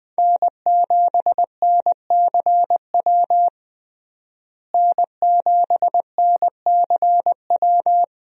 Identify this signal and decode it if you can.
Morse code, heard as a keyed tone, about 21 wpm, 700 Hz. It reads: N7NCW N7NCW
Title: USA Call Signs - 5 Characters - Repeated 2x - 20wpm